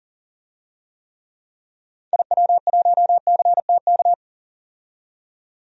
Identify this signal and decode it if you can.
Morse code, heard as a keyed tone, about 40 wpm, 700 Hz. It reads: IW1CTK
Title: DXCC-100 Call Signs - 6 Characters - 40wpm